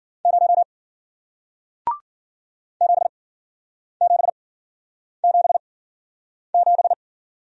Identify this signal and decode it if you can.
Morse code, heard as a keyed tone, about 53 wpm, spaced as for 9 wpm, 700 Hz. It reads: 0 7777